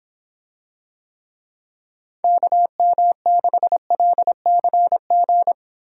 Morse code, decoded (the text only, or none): KM6LCG